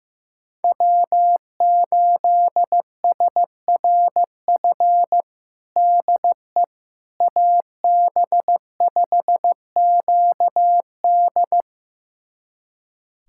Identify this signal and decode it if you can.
Morse code, heard as a keyed tone, about 15 wpm, 700 Hz. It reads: W8SRF DE AB5QD
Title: USA Callsign DE Callsign - 5 Characters - 15wpm